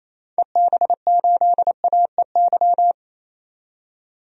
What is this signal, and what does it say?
Morse code, heard as a keyed tone, about 28 wpm, 700 Hz. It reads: EB8AEY
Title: DXCC-100 Call Signs - 6 Characters - 28wpm